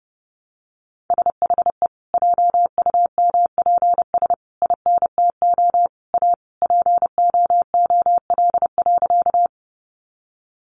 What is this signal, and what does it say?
Morse code, heard as a keyed tone, about 30 wpm, 700 Hz. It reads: SHE JUMPS INTO A POOL.
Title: Easy Phrases 30wpm